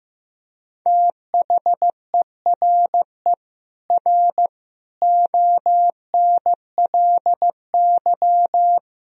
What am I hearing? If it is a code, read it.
Morse code, heard as a keyed tone, about 16 wpm, 700 Hz. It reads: THERE R ONLY
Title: Sentences from Top 200 Words and Abbreviations 15wpm